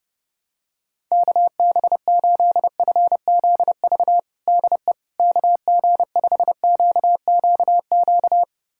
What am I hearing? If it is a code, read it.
Morse code, heard as a keyed tone, about 30 wpm, 700 Hz. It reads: KB8FZV DE KG5QQQ
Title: USA Callsign DE Callsign - 6 Characters - 30wpm